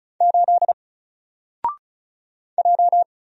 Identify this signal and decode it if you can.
Morse code, heard as a keyed tone, about 36 wpm, 700 Hz. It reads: 8 J
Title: Single Letter-Number Alt 35wpm